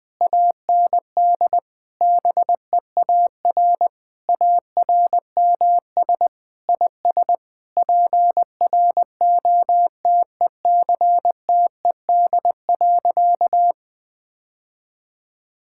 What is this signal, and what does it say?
Morse code, heard as a keyed tone, about 20 wpm, 700 Hz. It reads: AND BEAR ARMS IS PROTECTED.
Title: Sentences with 9-Word Phrases - 20wpm